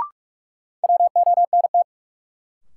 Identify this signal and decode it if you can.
Morse code, heard as a keyed tone, about 41 wpm, 700 Hz. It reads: WONT
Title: Sentences from Top 1000 Words - Review Part 6 - New Word Drill - 45wpm